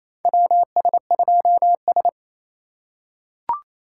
Morse code, heard as a keyed tone, about 28 wpm, 700 Hz. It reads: WS2S
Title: USA Call Signs - 4 Characters - 2x Character Spacing - 28wpm